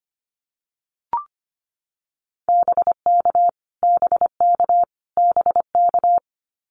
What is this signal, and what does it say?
Morse code, heard as a keyed tone, about 25 wpm, 700 Hz. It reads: BK BK BK
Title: Abbreviations Repeated 3x - 25wpm